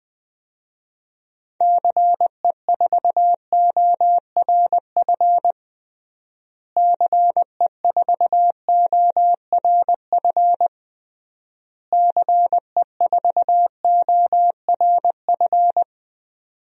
Morse code, heard as a keyed tone, about 21 wpm, 700 Hz. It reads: CE4ORF CE4ORF CE4ORF
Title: DXCC-100 Call Signs - 6 Characters - Repeated 3x - 20wpm